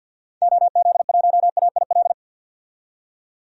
Morse code, heard as a keyed tone, about 49 wpm, 700 Hz. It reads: OZ1AIL
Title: DXCC-100 Call Signs - 6 Characters - 50wpm